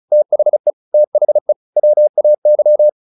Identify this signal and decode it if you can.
Morse code, heard as a keyed tone, about 34 wpm, 600 Hz. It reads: THE THE WAY